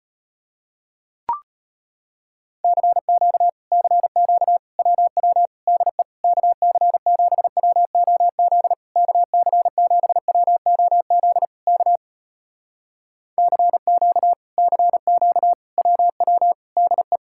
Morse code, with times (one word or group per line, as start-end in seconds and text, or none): 2.6-3.5 CQ
3.7-4.6 CQ
4.8-5.5 WW
5.7-6.0 DE
6.2-8.7 KC7WOZ
9.0-11.4 KC7WOZ
11.7-12.0 K
13.4-14.3 CQ
14.6-15.5 CQ
15.8-16.5 WW
16.8-17.2 DE